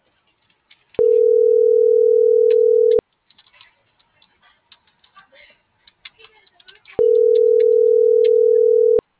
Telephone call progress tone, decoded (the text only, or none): ringback tone